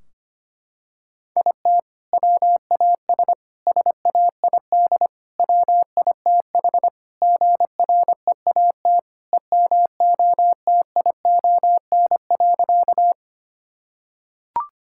Morse code, heard as a keyed tone, about 25 wpm, 700 Hz. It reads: IT WAS SAID WITH GREAT EMOTION.
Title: Sentences from Top 1600 Words 25wpm